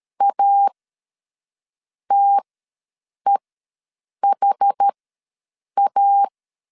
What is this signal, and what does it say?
Morse code, heard as a keyed tone, about 13 wpm, spaced as for 6 wpm, 800 Hz. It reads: ATEHA
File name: LEZIONI / 5 LBVC / 5 LBVC ws 12wpm 3159